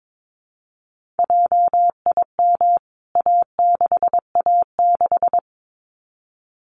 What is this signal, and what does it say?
Morse code, heard as a keyed tone, about 22 wpm, 700 Hz. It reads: JIM A6A6
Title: CWT Exchange 22wpm